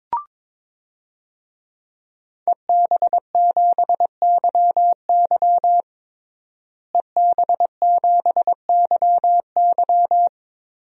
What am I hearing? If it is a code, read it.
Morse code, heard as a keyed tone, about 22 wpm, 700 Hz. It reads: EB7YY EB7YY
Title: DXCC-100 Call Signs - 5 Characters - Repeated 3x - 22wpm